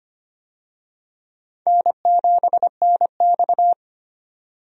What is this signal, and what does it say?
Morse code, heard as a keyed tone, about 25 wpm, 700 Hz. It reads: N7NX
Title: USA Call Signs - 4 Characters - 25wpm